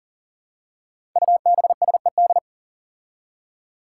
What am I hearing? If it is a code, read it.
Morse code, heard as a keyed tone, about 40 wpm, 700 Hz. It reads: UBSED
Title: Five Challenging Letter-Number Combinations 40wpm